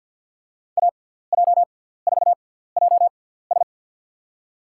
Morse code, heard as a keyed tone, about 47 wpm, spaced as for 17 wpm, 690 Hz. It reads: AJ4JS
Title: USA Call Signs - 5 Characters - 6x Character Spacing - 50wpm